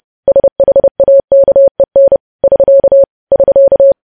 Morse code, heard as a keyed tone, about 30 wpm, 550 Hz. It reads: SHAKEN <SK> <SK>
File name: podcast / News headlines-1x-30x30